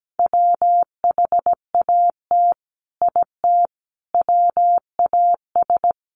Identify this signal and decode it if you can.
Morse code, heard as a keyed tone, about 17 wpm, 700 Hz. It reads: WHAT IT WAS
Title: Sets of 5 Words Encore 17wpm